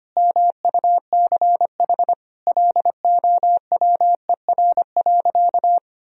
Morse code, heard as a keyed tone, about 25 wpm, 700 Hz. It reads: MUCH LOWER.
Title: Sentences from Top 1000 Words - Review Part 4 - 25wpm